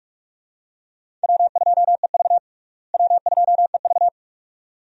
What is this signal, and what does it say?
Morse code, heard as a keyed tone, about 41 wpm, 700 Hz. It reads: W2EV W2EV
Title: USA Call Signs - 4 Characters - Repeated 2x - 45wpm